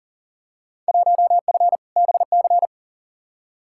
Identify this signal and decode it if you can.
Morse code, heard as a keyed tone, about 40 wpm, 700 Hz. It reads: 1F BC
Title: Field Day Exchanges Encore - 40wpm